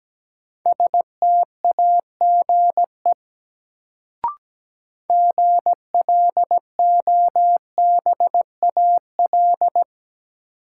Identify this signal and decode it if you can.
Morse code, heard as a keyed tone, about 17 wpm, 700 Hz. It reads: STAGE GLOBAL
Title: Sentences from Top 1200 Words - New Word Drill Alt - 17wpm